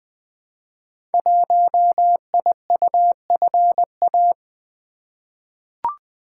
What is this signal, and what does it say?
Morse code, heard as a keyed tone, about 20 wpm, 700 Hz. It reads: 1IUFA
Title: Code Groups - 5 Letter:Numbers - 20wpm